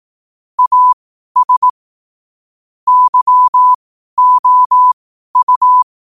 Morse code, heard as a keyed tone, about 18 wpm, spaced as for 12 wpm, 1000 Hz. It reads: AS YOU
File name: qotdfiles / 12wpm